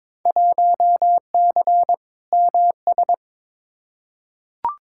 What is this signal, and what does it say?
Morse code, heard as a keyed tone, about 22 wpm, 700 Hz. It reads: 1C MS
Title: Field Day Exchanges Encore - 22wpm